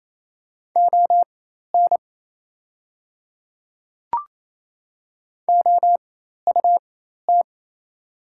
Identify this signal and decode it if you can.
Morse code, heard as a keyed tone, about 28 wpm, spaced as for 13 wpm, 700 Hz. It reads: ON OUT
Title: 4x Character Spacing - Top 100 Words - 28wpm